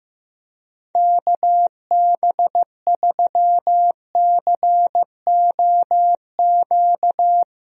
Morse code, heard as a keyed tone, about 15 wpm, 700 Hz. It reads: KB3COQ
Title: USA Call Signs - 6 Characters - 15wpm